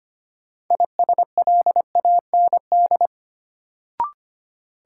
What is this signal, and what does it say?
Morse code, heard as a keyed tone, about 25 wpm, 700 Hz. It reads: ISLAND
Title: Sentences from Top 500 Words - New Word Drill - 25wpm